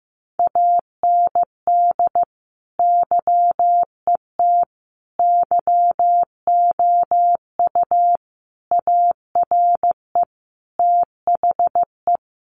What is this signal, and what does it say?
Morse code, heard as a keyed tone, about 14 wpm, 700 Hz. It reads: AND YET YOU ARE THE
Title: Sentences from Top 500 Words 15wpm